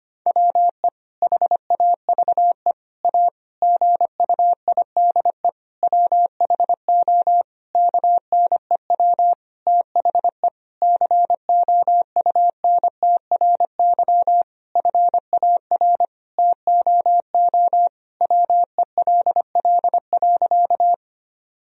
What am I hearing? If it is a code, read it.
Morse code, heard as a keyed tone, about 25 wpm, 700 Hz. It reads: WE HAVE A GUIDE WHO KNEW THE COUNTRY FAR TOO WELL.
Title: Sentences from Top 1000 Words - Review Part 3 - 25wpm